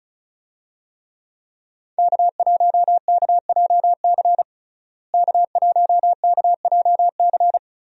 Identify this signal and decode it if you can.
Morse code, heard as a keyed tone, about 36 wpm, 700 Hz. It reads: K1KJC K1KJC
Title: USA Call Signs - 5 Characters - Repeated 3x - 35wpm